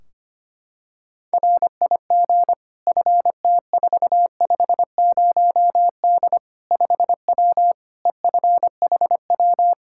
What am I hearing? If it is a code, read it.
Morse code, heard as a keyed tone, about 25 wpm, 700 Hz. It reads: RIG FT450D 5W EFHW
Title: Practice QSO - Exchange 2 - Info - 25wpm